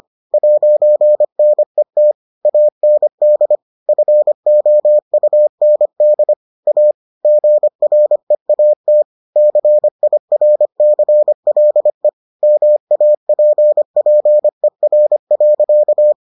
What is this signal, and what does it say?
Morse code, heard as a keyed tone, about 25 wpm, 600 Hz. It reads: 'NET AND FOUND A GREAT CIRCLE MAPPER.